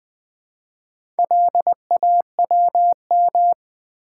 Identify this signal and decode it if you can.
Morse code, heard as a keyed tone, about 20 wpm, 700 Hz. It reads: LAWM
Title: Build A Word - 2 Letters At A Time - 8-Letter Words - 20wpm